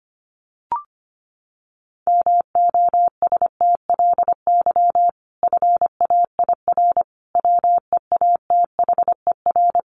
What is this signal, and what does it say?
Morse code, heard as a keyed tone, about 25 wpm, 700 Hz. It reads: MOSTLY FAIR WEATHER
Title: Sentences from Top 1300 Words 25wpm